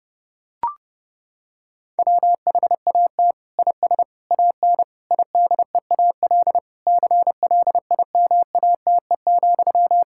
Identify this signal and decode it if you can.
Morse code, heard as a keyed tone, about 30 wpm, 700 Hz. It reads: WHAT IS AN IDEAL CLIMATE,
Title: Sentences from Top 1600 Words 30wpm